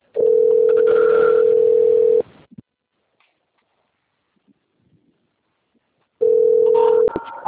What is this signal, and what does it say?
Ringback tone (North America), 440 + 480 Hz.
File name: rg-1600-unknown- - - .8129